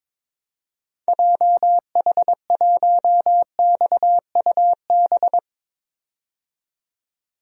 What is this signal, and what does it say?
Morse code, heard as a keyed tone, about 22 wpm, 700 Hz. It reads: JH1XUB
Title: DXCC-100 Call Signs - 6 Characters - 22wpm